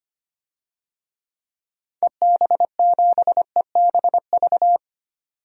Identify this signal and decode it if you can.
Morse code, heard as a keyed tone, about 25 wpm, 700 Hz. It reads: EB7EBV
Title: DXCC-100 Call Signs - 6 Characters - 25wpm